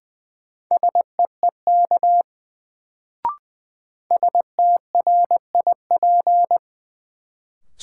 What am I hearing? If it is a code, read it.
Morse code, heard as a keyed tone, about 20 wpm, 700 Hz. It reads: SEEK STRIP
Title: Sentences from Top 1600 Words - New Word Drill - 20wpm